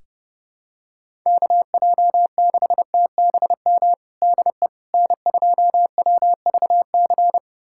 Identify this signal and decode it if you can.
Morse code, heard as a keyed tone, about 32 wpm, 700 Hz. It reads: KJ6TBM DE N2WVC
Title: USA Callsign DE Callsign 30wpm